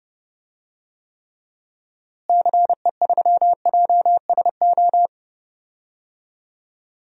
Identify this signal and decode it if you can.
Morse code, heard as a keyed tone, about 30 wpm, 700 Hz. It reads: CE3JSO
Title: DXCC-100 Call Signs - 6 Characters - Repeated 2x - 30wpm